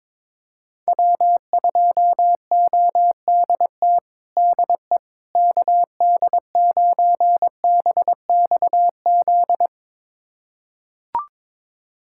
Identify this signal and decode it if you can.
Morse code, heard as a keyed tone, about 22 wpm, 700 Hz. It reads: W2ODT DE KD9BXZ
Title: USA Callsign DE Callsign 22wpm